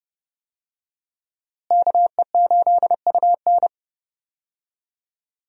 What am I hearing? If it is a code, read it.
Morse code, heard as a keyed tone, about 30 wpm, 700 Hz. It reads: KE8UN
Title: USA Call Signs - 5 Characters - Repeated 2x - 30wpm